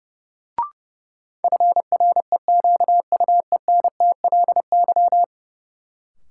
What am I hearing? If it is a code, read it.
Morse code, heard as a keyed tone, about 30 wpm, 700 Hz. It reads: FREQUENTLY
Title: Sentences from Top 1300 Words - New Word Drill - 30wpm